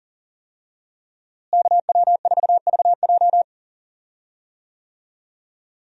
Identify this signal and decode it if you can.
Morse code, heard as a keyed tone, about 40 wpm, 700 Hz. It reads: KW4VJ
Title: USA Call Signs - 5 Characters - Repeated 3x - 40wpm